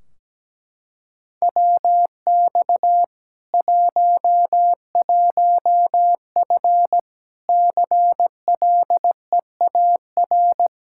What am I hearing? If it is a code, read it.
Morse code, heard as a keyed tone, about 17 wpm, 700 Hz. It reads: WX 11F CLEAR
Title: Practice QSO - Exchange 2 - Info - 17wpm